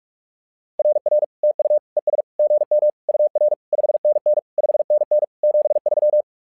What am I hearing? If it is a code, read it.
Morse code, heard as a keyed tone, about 46 wpm, 600 Hz. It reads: RR TU ES GM UR 5NN 5NN 73
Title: SOTA QSO Practice 45wpm